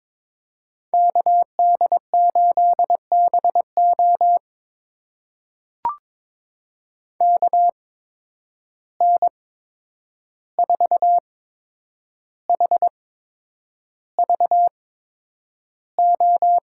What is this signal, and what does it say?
Morse code, heard as a keyed tone, about 22 wpm, spaced as for 9 wpm, 700 Hz. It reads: KD8BO KN4HVO
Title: USA Call Signs - 8x Character Spacing - 22wpm